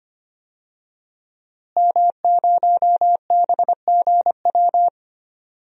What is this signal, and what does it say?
Morse code, heard as a keyed tone, about 25 wpm, 700 Hz. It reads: M0BGW
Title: DXCC-100 Call Signs - 5 Characters - Repeated 2x - 25wpm